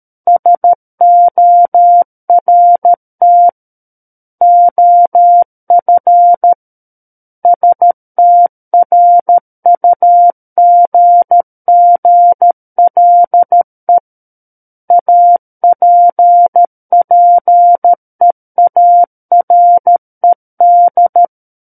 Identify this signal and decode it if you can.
Morse code, heard as a keyed tone, about 13 wpm, 700 Hz. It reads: SORT OF STRUGGLE APPEARED
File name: Book One / 04-Chapter 4 - 13 WPM